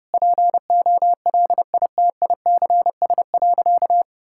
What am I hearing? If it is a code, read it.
Morse code, heard as a keyed tone, about 30 wpm, 700 Hz. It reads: POLITICS.
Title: Sentences from Top 1000 Words - Review Part 6 - 30wpm